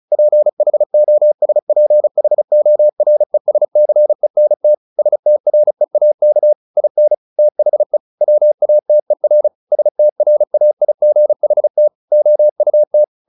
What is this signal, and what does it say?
Morse code, heard as a keyed tone, about 35 wpm, 600 Hz. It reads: PHOSPHORESCENT STREAK IN THE WATER STRAIGHT OUT
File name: the old man and the sea / the old man and the sea 1 35wpm